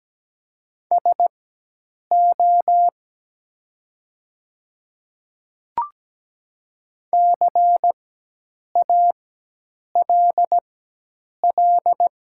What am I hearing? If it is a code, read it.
Morse code, heard as a keyed tone, about 17 wpm, spaced as for 8 wpm, 700 Hz. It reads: SO CALL
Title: 4x Character Spacing - Top 200 Words - 17wpm